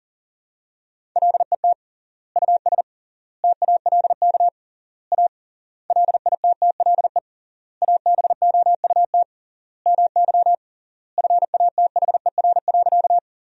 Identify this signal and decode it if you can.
Morse code, heard as a keyed tone, about 40 wpm, 700 Hz. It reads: LET US TALK A LITTLE ABOUT MY FATHER.
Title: 3x Word-Spacing - Sentences from Top 400 Words - 40wpm